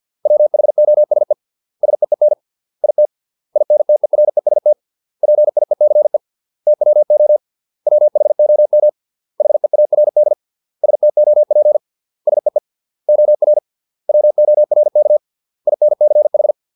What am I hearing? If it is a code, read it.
Morse code, heard as a keyed tone, about 50 wpm, 600 Hz. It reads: WHOSE SEEN IT INTEREST PIECE TWO WHOM HEARD STOP SEE OR WORK INCH